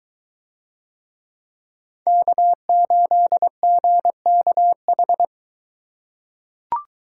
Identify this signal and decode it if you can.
Morse code, heard as a keyed tone, about 20 wpm, 700 Hz. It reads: K8GKH